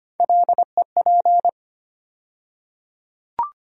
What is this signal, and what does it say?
Morse code, heard as a keyed tone, about 25 wpm, 700 Hz. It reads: LEP
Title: USA Call Sign Postfixes - 3 Letters 25wpm